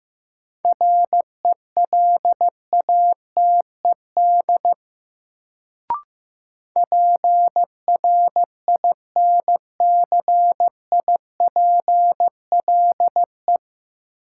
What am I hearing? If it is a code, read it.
Morse code, heard as a keyed tone, about 15 wpm, 700 Hz. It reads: RELATED PRINCIPLE
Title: Sentences from Top 1400 Words - New Word Drill Alt - 15wpm